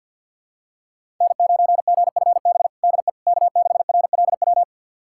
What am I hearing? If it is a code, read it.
Morse code, heard as a keyed tone, about 53 wpm, 700 Hz. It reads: N9GFB DE K6RLW
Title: USA Callsign DE Callsign - 5 Characters - 50wpm